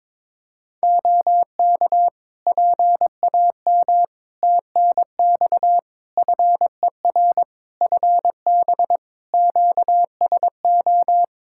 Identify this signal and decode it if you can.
Morse code, heard as a keyed tone, about 21 wpm, 700 Hz. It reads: OK PAM TNX FER FB QSO
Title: K1USN SST QSO Practice 22wpm